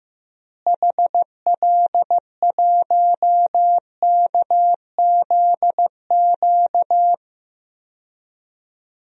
Morse code, heard as a keyed tone, about 16 wpm, 700 Hz. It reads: HL1KZQ
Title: DXCC-100 Call Signs - 6 Characters - Repeated 2x - 15wpm